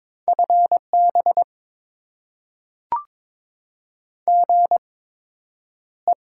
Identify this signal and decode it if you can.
Morse code, heard as a keyed tone, about 22 wpm, spaced as for 9 wpm, 700 Hz. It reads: FB GE
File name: Top 20 QSO Elements - 8x Character Spacing - 22wpm